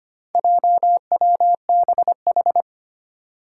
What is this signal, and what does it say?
Morse code, heard as a keyed tone, about 25 wpm, 700 Hz. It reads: JWBH
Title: Four Challenging Letter-Number Combinations 25wpm